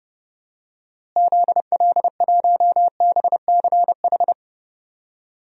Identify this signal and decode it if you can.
Morse code, heard as a keyed tone, about 31 wpm, 700 Hz. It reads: ZL1BCH
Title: DXCC-100 Call Signs - 6 Characters - Repeated 2x - 30wpm